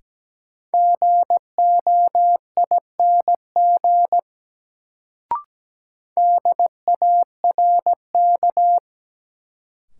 Morse code, heard as a keyed tone, about 17 wpm, 700 Hz. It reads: GOING DARK
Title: Sentences from Top 500 Words - New Word Drill - 17wpm